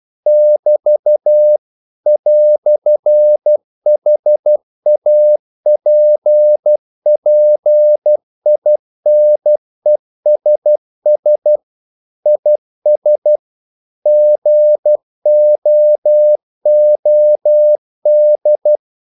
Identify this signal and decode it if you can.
Morse code, heard as a keyed tone, about 12 wpm, 600 Hz. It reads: <BT> "HAPPINESS IS GOOD
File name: CW / Practice / EnglishQuotations0312